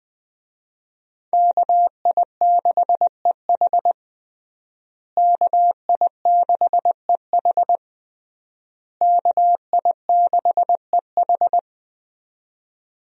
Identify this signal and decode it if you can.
Morse code, heard as a keyed tone, about 20 wpm, 700 Hz. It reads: KI6EH KI6EH KI6EH
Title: USA Call Signs - 5 Characters - Repeated 3x - 20wpm